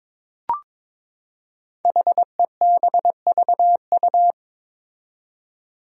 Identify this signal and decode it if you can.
Morse code, heard as a keyed tone, about 21 wpm, 700 Hz. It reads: HEBVU
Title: Five Challenging Letter-Number Combinations 22wpm